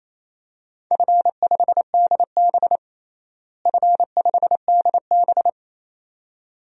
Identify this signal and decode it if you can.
Morse code, heard as a keyed tone, about 28 wpm, 700 Hz. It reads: F5DB F5DB
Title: DXCC-100 Call Signs - 4 Characters - Repeated 2x - 28wpm